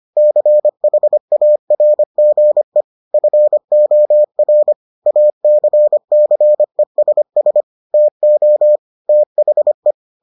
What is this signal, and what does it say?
Morse code, heard as a keyed tone, about 25 wpm, 600 Hz. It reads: CHARGE FOR ACCESS TO THE